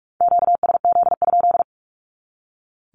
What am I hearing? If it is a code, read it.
Morse code, heard as a keyed tone, about 44 wpm, 700 Hz. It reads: QSZ?
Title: Q-Codes 45wpm